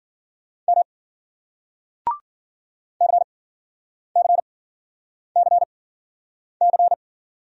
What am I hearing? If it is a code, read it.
Morse code, heard as a keyed tone, about 53 wpm, spaced as for 9 wpm, 700 Hz. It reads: M CCCC